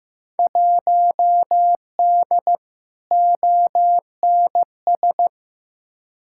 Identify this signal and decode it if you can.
Morse code, heard as a keyed tone, about 15 wpm, 700 Hz. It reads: 1D ONS
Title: Field Day Exchanges - 15wpm